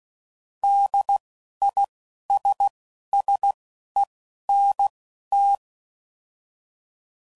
Morse code, heard as a keyed tone, about 16 wpm, spaced as for 11 wpm, 800 Hz. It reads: DISSENT